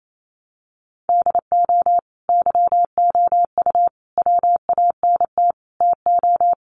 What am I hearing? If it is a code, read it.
Morse code, heard as a keyed tone, about 28 wpm, 700 Hz. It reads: DO YOU WANT TO
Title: Sets of 5 Words Encore2 28wpm